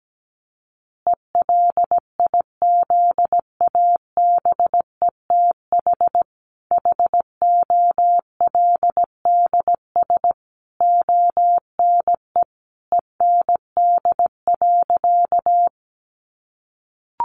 Morse code, heard as a keyed tone, about 17 wpm, 700 Hz. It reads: ELIZABETH HOLDS ONE END.
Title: Easy Phrases 17wpm